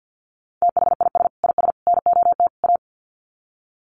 Morse code, heard as a keyed tone, about 50 wpm, 700 Hz. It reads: THIS IS NOT A
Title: Sets of 4 Words 50wpm